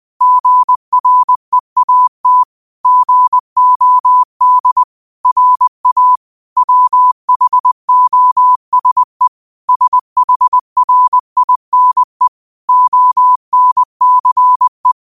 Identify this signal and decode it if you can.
Morse code, heard as a keyed tone, about 20 wpm, 1000 Hz. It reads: GREAT GOD RA WHOSE SHRINE ONCE
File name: qotdfiles / 20wpm